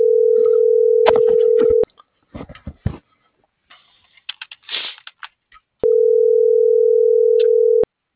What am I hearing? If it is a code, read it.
Ringback tone (North America), 440 + 480 Hz.